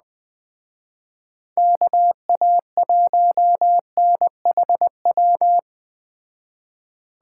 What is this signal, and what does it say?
Morse code, heard as a keyed tone, about 20 wpm, 700 Hz. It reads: KA1NHW
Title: Speed Racing - USA 2x3 Call Signs 17wpm